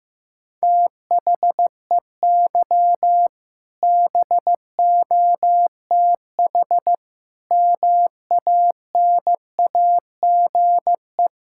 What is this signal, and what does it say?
Morse code, heard as a keyed tone, about 15 wpm, 700 Hz. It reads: THEY BOTH MANAGE
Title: Sentences from Top 1600 Words 15wpm